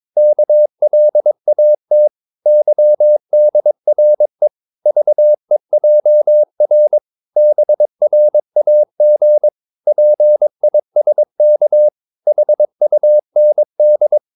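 Morse code, heard as a keyed tone, about 22 wpm, 600 Hz. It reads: KLAT YDRE VEJR BRAG PISK HUND
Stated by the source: Morse practice of short Danish words at 22 wpm, from CW med Gnister.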